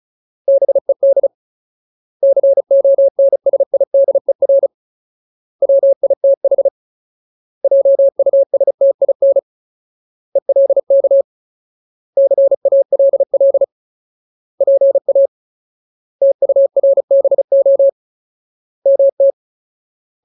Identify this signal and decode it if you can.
Morse code, heard as a keyed tone, about 35 wpm, 550 Hz. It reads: BED CONSIDER WITH JUSTIN ELK CALL PA TURBO MT
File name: Ultimate Word List / 35WPM Ultimate